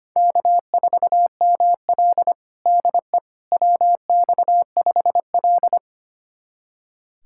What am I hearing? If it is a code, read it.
Morse code, heard as a keyed tone, about 25 wpm, 700 Hz. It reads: K4ML DE WX5L
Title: USA Callsign DE Callsign - 4 Characters - 25wpm